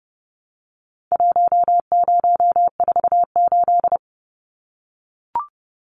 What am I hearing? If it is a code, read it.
Morse code, heard as a keyed tone, about 30 wpm, 700 Hz. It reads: 1048
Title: Numbers - Contest Style - 4-Digit 30wpm